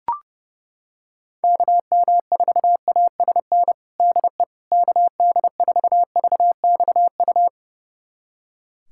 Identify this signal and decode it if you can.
Morse code, heard as a keyed tone, about 32 wpm, 700 Hz. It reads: KM4ASN DE KD4VXU
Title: USA Callsign DE Callsign - 6 Characters - 30wpm